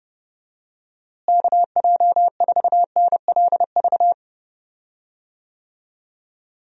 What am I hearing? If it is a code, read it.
Morse code, heard as a keyed tone, about 31 wpm, 700 Hz. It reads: KJ4NLV
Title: USA Call Signs - 6 Characters - Repeated 2x - 30wpm